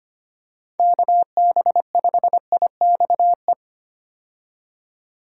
Morse code, heard as a keyed tone, about 25 wpm, 700 Hz. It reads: KB5IXE
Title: USA Call Signs - 6 Characters - 25wpm